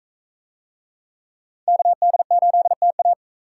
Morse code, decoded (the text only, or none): KD8TA